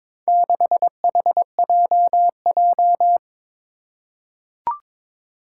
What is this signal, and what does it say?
Morse code, heard as a keyed tone, about 22 wpm, 700 Hz. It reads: 6HJJ
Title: Four Challenging Letter-Number Combinations 22wpm